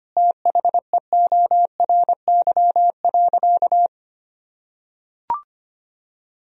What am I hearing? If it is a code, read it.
Morse code, heard as a keyed tone, about 25 wpm, 700 Hz. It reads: THEORY.
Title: Sentences from Top 1000 Words - Review Part 3 - 25wpm